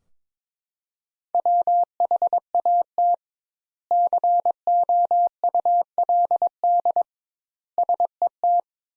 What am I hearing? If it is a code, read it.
Morse code, heard as a keyed tone, about 22 wpm, 700 Hz. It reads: WHAT COULD SET
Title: Sentences Top 200 Words - 22wpm